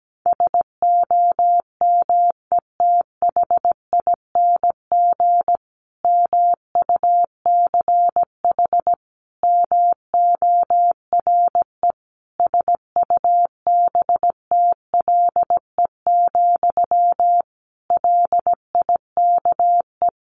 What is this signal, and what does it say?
Morse code, heard as a keyed tone, about 17 wpm, 700 Hz. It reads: SOMETHING MUCH MORE SUBTLE, LIKE
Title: A Sound Of Thunder 17wpm